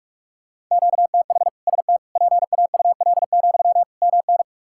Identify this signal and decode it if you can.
Morse code, heard as a keyed tone, about 46 wpm, 700 Hz. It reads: QTH ST PAUL, MN
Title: K1USN SST QSO Practice 45wpm